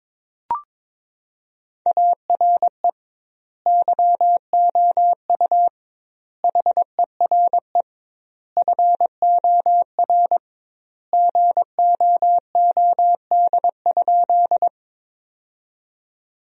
Morse code, heard as a keyed tone, about 22 wpm, 700 Hz. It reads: ARE YOU HERE FOR GOOD?
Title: 2x Word-Spacing - Sentences from Top 200 Words - 22wpm